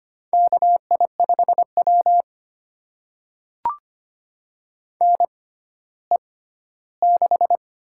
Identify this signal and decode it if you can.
Morse code, heard as a keyed tone, about 25 wpm, spaced as for 14 wpm, 700 Hz. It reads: KI5W NE6
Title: USA Call Signs - 4 Characters - 6x Character Spacing - 25wpm